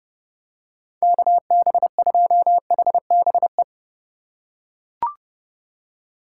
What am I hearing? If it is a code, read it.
Morse code, heard as a keyed tone, about 30 wpm, 700 Hz. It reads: KB2HBE
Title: USA Call Signs - 6 Characters - 4x Character Spacing - 30wpm